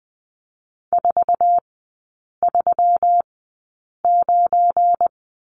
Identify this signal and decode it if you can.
Morse code, heard as a keyed tone, about 20 wpm, spaced as for 8 wpm, 690 Hz. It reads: 439
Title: 212 Lesson 34e - Number 8 - Three Characters 20wpm